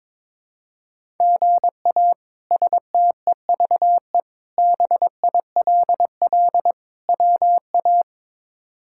Morse code, heard as a keyed tone, about 22 wpm, 700 Hz. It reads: GA STEVE BILL WA
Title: K1USN SST QSO Practice - 22wpm